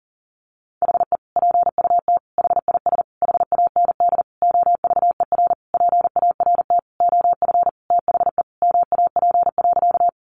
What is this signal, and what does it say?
Morse code, heard as a keyed tone, about 40 wpm, 700 Hz. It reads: HE PUT HIS HAND OVER PART OF THE MAP.
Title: Sentences from Top 400 Words 40wpm